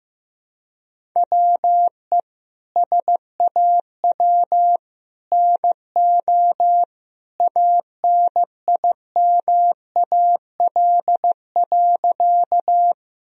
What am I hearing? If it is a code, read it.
Morse code, heard as a keyed tone, about 15 wpm, 700 Hz. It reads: WE SAW NO ANIMAL.
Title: Sentences from Top 1000 Words - Review Part 1 - 15wpm